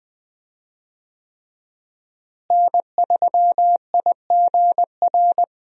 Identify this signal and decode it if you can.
Morse code, heard as a keyed tone, about 21 wpm, 700 Hz. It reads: N3IGR
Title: USA Call Signs - 5 Characters - Repeated 3x - 20wpm